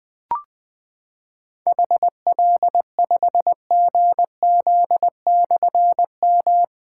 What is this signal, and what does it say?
Morse code, heard as a keyed tone, about 20 wpm, 700 Hz. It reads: HL5GZ/M
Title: Difficult Call Signs 20wpm